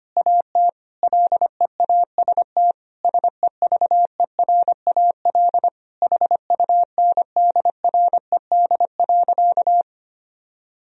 Morse code, heard as a keyed tone, about 25 wpm, 700 Hz. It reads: AT LEAST SEVERAL HUNDRED.
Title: Sentences from Top 1000 Words - Review Part 2 - 25wpm